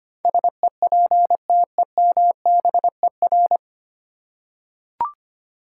Morse code, heard as a keyed tone, about 25 wpm, 700 Hz. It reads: SEPTEMBER
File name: practice / 25wpm / 9 Letter Words - First 3 Letters Given - 25wpm